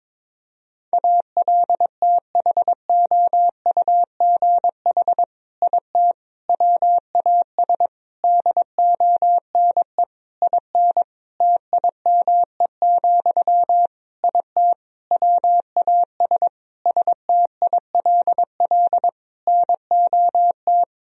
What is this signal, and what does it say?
Morse code, heard as a keyed tone, about 22 wpm, 700 Hz. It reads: ALTHOUGH IT WAS DONE IN TIME, IT WAS STILL NOT
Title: Sentences from Top 1000 Words - Review Part 3 - 22wpm